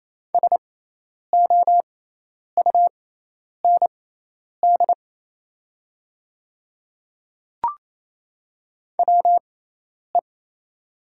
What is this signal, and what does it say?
Morse code, heard as a keyed tone, about 28 wpm, spaced as for 10 wpm, 700 Hz. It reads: SOUND WE
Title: 6x Character Spacing - Top 100 Words - 28wpm